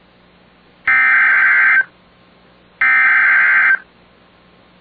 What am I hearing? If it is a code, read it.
Emergency Alert System (SAME) tones (header bursts), heard as harsh buzzing bursts. It is heard from 0.9 to 3.7 s.